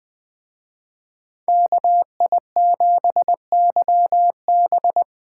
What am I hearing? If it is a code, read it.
Morse code, heard as a keyed tone, about 19 wpm, 700 Hz. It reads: KI7YB